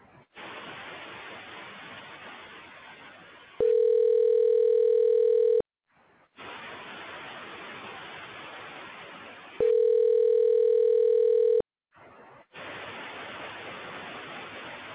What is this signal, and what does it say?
Ringback tone (North America), 440 + 480 Hz.